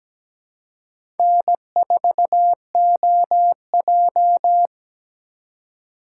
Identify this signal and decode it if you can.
Morse code, heard as a keyed tone, about 17 wpm, 700 Hz. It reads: N4OJ
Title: USA Call Signs - 2x Character Spacing - 17wpm